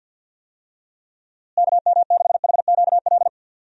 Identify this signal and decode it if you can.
Morse code, heard as a keyed tone, about 50 wpm, 700 Hz. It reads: KM6HQL
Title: USA Call Signs - 6 Characters - Repeated 3x - 50wpm